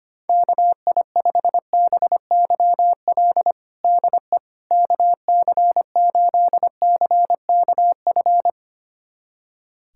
Morse code, heard as a keyed tone, about 25 wpm, 700 Hz. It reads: KI5BYL DE KC8CKF
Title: USA Callsign DE Callsign - 6 Characters - 25wpm